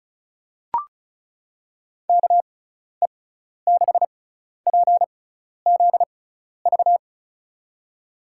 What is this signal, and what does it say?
Morse code, heard as a keyed tone, about 34 wpm, spaced as for 12 wpm, 700 Hz. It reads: KE6PZV
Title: USA Call Signs - 6 Characters - 6x Character Spacing - 35wpm